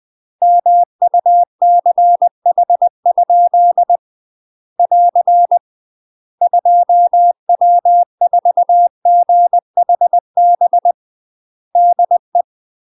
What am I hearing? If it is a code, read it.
Morse code, heard as a keyed tone, about 20 wpm, 700 Hz. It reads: MUCH? <AR> 2W4GHB DE
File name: UK MASTER QSOs TRAINING / 20WPM / UK MASTER QSOs 0055